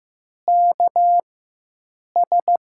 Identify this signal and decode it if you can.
Morse code, heard as a keyed tone, about 15 wpm, spaced as for 7 wpm, 700 Hz. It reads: KS
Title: USA Call Signs - 6 Characters - 4x Character Spacing - 15wpm